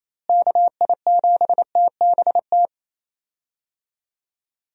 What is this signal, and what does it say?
Morse code, heard as a keyed tone, about 28 wpm, 700 Hz. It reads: KI7TBT
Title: USA Call Signs - 6 Characters - Repeated 3x - 28wpm